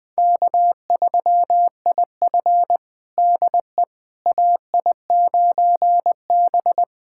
Morse code, heard as a keyed tone, about 20 wpm, 700 Hz. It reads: K3IF DE AI9B
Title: USA Callsign DE Callsign - 4 Characters - 20wpm